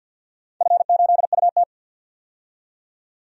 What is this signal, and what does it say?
Morse code, heard as a keyed tone, about 52 wpm, 700 Hz. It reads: F8UT
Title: DXCC-100 Call Signs - 4 Characters - 50wpm